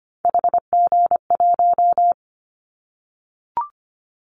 Morse code, heard as a keyed tone, about 25 wpm, 700 Hz. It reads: HG1
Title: Callsign Prefixes for Top 25 DXCC - Repeated 3x - 25wpm